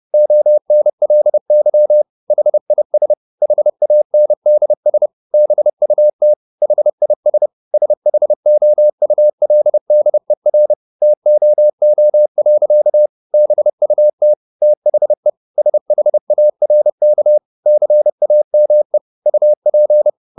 Morse code, heard as a keyed tone, about 30 wpm, 600 Hz. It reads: ONLY HIS HANDS BUT HIS SHOULDER TOO. BUT THE SHARK CAME UP